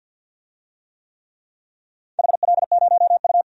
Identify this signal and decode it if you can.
Morse code, heard as a keyed tone, about 52 wpm, 700 Hz. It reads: HL0U
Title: DXCC-100 Call Signs - 4 Characters - 50wpm